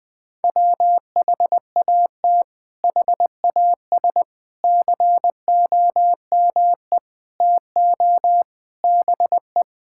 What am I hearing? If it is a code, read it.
Morse code, heard as a keyed tone, about 20 wpm, 700 Hz. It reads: WHAT HAS COME TO BE
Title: Sets of 8 Words - 20wpm